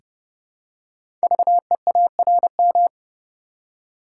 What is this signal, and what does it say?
Morse code, heard as a keyed tone, about 30 wpm, 700 Hz. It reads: VEARM
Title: Code Groups - 5 Letters - 30wpm